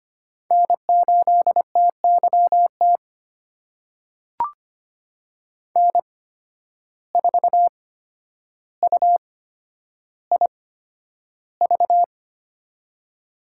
Morse code, heard as a keyed tone, about 25 wpm, spaced as for 11 wpm, 700 Hz. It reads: N8TYT N4UIV
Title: USA Call Signs - 5 Characters - 8x Character Spacing - 25wpm